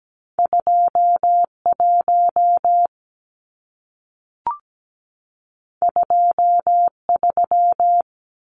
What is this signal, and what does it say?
Morse code, heard as a keyed tone, about 17 wpm, 700 Hz. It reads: 21 23
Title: Numbers - 2-Digit 17wpm